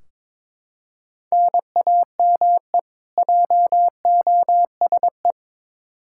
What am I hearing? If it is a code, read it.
Morse code, heard as a keyed tone, about 22 wpm, 700 Hz. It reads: NAME JOSE
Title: Practice QSO - Exchange 1 - Beginning - 22wpm